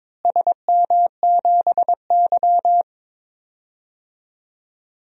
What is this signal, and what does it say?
Morse code, heard as a keyed tone, about 21 wpm, 700 Hz. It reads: SM7Y
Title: DXCC-100 Call Signs - 4 Characters - Repeated 3x - 22wpm